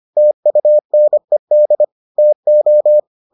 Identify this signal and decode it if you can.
Morse code, heard as a keyed tone, about 25 wpm, 600 Hz. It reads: TUNED TO